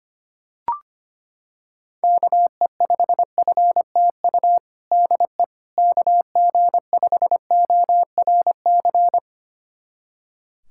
Morse code, heard as a keyed tone, about 25 wpm, 700 Hz. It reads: KE5FTU DE KG5ORC
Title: USA Callsign DE Callsign 25wpm